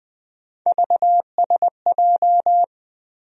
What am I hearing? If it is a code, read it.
Morse code, heard as a keyed tone, about 20 wpm, 700 Hz. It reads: VSJ
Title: USA Call Sign Postfixes - 3 Letters 20wpm